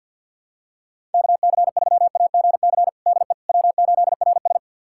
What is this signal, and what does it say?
Morse code, heard as a keyed tone, about 51 wpm, 700 Hz. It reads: KC3AGC DE W8RS
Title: USA Callsign DE Callsign 50wpm